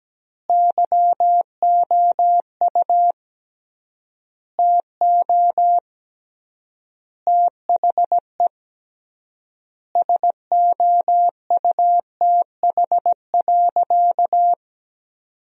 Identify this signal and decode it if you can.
Morse code, heard as a keyed tone, about 17 wpm, 700 Hz. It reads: YOU TO THE SOUTH.
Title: 3x Word-Spacing - Sentences from Top 400 Words - 17wpm